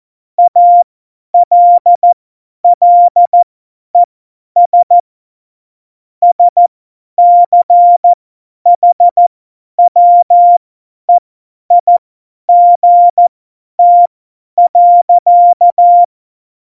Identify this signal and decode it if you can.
Morse code, heard as a keyed tone, about 14 wpm, spaced as for 10 wpm, 700 Hz. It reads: ALLES SCHWEIGT.
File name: Morsekurs / Witze / Witz 04. Juni 2019, 50 F70